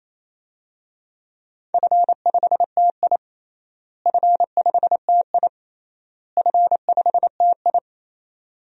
Morse code, heard as a keyed tone, about 28 wpm, 700 Hz. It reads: F5TI F5TI F5TI
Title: DXCC-100 Call Signs - 4 Characters - Repeated 3x - 28wpm